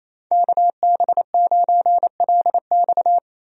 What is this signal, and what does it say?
Morse code, heard as a keyed tone, about 28 wpm, 700 Hz. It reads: KB9LX
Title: USA Call Signs - 5 Characters - Repeated 2x - 28wpm